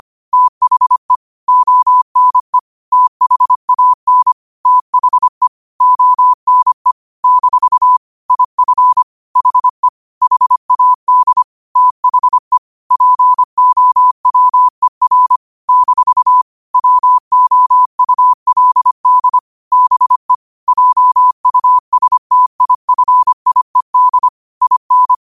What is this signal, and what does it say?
Morse code, heard as a keyed tone, about 25 wpm, 1000 Hz. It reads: THE ONE THAN THE ONE - IF HE HAD THE POWER - WOULD BE JUSTIFIED IN
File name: qotdfiles / 25wpm